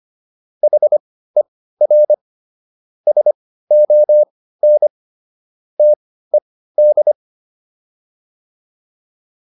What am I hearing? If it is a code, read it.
Morse code, heard as a keyed tone, about 25 wpm, spaced as for 15 wpm, 600 Hz. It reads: HER SON TED